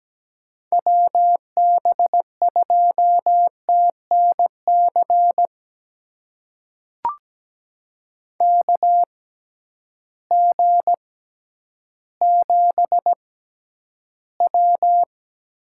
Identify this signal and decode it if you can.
Morse code, heard as a keyed tone, about 17 wpm, spaced as for 10 wpm, 700 Hz. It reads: WB2TNC KG7W
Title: USA Call Signs - 6 Characters - 6x Character Spacing - 17wpm